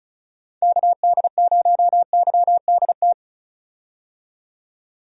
Morse code, heard as a keyed tone, about 35 wpm, 700 Hz. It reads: KD0YDT
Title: USA Call Signs - 6 Characters - 35wpm